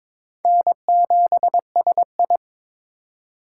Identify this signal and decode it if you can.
Morse code, heard as a keyed tone, about 26 wpm, 700 Hz. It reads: N7SI
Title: Speed Racing - USA 1x2 Call Signs 22wpm